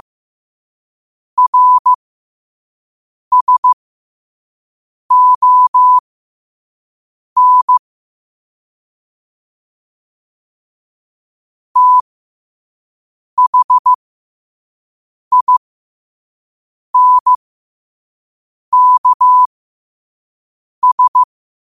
Morse code, heard as a keyed tone, about 15 wpm, spaced as for 5 wpm, 1000 Hz. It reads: RSON THINKS
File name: qotdfiles / 5wpm